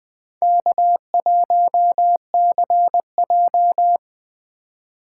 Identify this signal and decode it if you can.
Morse code, heard as a keyed tone, about 20 wpm, 700 Hz. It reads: K1CJ
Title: Speed Racing - USA 1x2 Call Signs 17wpm